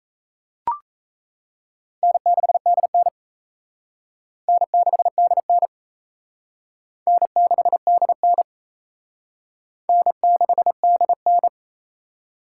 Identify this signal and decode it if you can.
Morse code, heard as a keyed tone, about 34 wpm, 700 Hz. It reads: N6DN N6DN N6DN N6DN
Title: Speed Racing - USA 1x2 Call Signs 28wpm